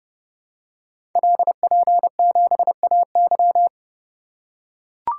Morse code, heard as a keyed tone, about 29 wpm, 700 Hz. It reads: LP7AY
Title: Code Groups - 5 Letter:Numbers - 30wpm